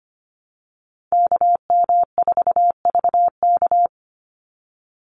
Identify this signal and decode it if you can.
Morse code, heard as a keyed tone, about 25 wpm, 690 Hz. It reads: KM4VK
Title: USA Call Signs 25wpm